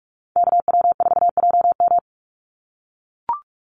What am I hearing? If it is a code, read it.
Morse code, heard as a keyed tone, about 44 wpm, 700 Hz. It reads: KW4JM
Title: NA SOTA Activator Callsigns 45wpm